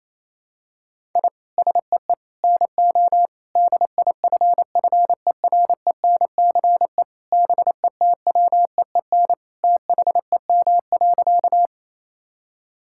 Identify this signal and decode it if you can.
Morse code, heard as a keyed tone, about 28 wpm, 700 Hz. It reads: I SEE NO DIFFERENCE BETWEEN THEM.
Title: Sentences from Top 1000 Words - Review Part 5 - 28wpm